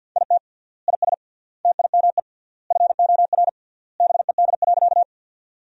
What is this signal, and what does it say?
Morse code, heard as a keyed tone, about 50 wpm, 700 Hz. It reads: IT IS TIME FOR BED.
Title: 3x Word-Spacing - Sentences from Top 500 Words - 50wpm